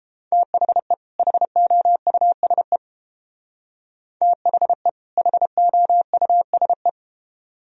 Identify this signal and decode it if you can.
Morse code, heard as a keyed tone, about 29 wpm, 700 Hz. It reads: THE HOUSE THE HOUSE